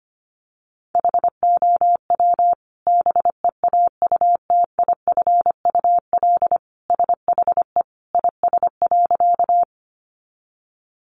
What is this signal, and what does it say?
Morse code, heard as a keyed tone, about 25 wpm, 700 Hz. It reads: HOW BEAUTIFUL SHE IS.
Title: Sentences from Top 900 Words 25wpm